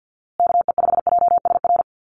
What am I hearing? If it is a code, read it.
Morse code, heard as a keyed tone, about 50 wpm, 700 Hz. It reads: KE5JSR
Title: USA Call Signs Repeated 3x 50wpm